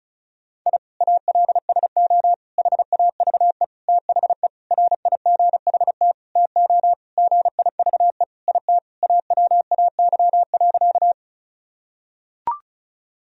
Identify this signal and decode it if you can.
Morse code, heard as a keyed tone, about 34 wpm, 700 Hz. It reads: I ALSO HAVE THE RIGHT TO GIVE IT AWAY.
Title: Sentences from Top 1000 Words - Review Part 1 - 35wpm